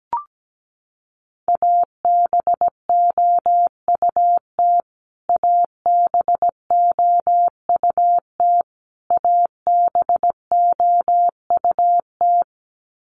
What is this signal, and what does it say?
Morse code, heard as a keyed tone, about 17 wpm, 700 Hz. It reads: ABOUT ABOUT ABOUT
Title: Top 200 Words Repeated 17wpm